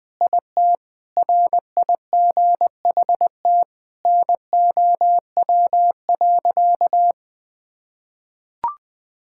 Morse code, heard as a keyed tone, about 20 wpm, 700 Hz. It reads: IT RIGHT NOW.
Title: Sentences with 5-Word Phrases - 20wpm